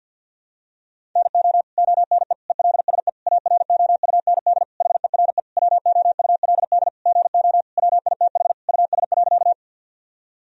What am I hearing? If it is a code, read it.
Morse code, heard as a keyed tone, about 49 wpm, 700 Hz. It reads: NO ONE ELSE AROUND HERE WOULD GO WITH US.
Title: Sentences from Top 1000 Words - Review Part 3 - 50wpm